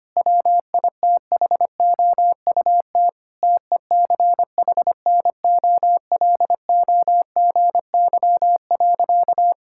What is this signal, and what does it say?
Morse code, heard as a keyed tone, about 25 wpm, 700 Hz. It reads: WITHOUT TECHNOLOGY.
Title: Sentences from Top 1000 Words - Review Part 5 - 25wpm